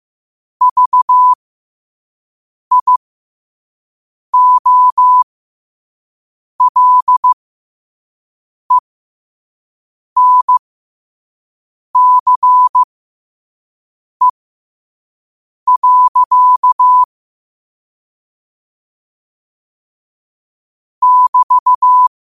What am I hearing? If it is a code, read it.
Morse code, heard as a keyed tone, about 15 wpm, spaced as for 5 wpm, 1000 Hz. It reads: VIOLENCE. <BT>
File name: qotdfiles / 5wpm